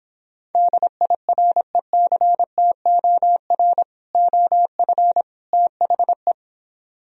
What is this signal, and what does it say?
Morse code, heard as a keyed tone, about 26 wpm, 700 Hz. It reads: DIRECTOR OF THE